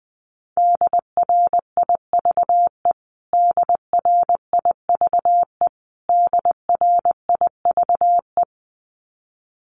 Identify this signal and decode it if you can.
Morse code, heard as a keyed tone, about 20 wpm, 700 Hz. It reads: DRIVE DRIVE DRIVE
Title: Top 500 Words Repeated 20wpm